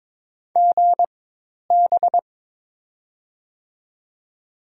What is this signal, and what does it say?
Morse code, heard as a keyed tone, about 22 wpm, spaced as for 10 wpm, 700 Hz. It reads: GB
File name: Top 90 QSO Elements - 4x Character Spacing - 22wpm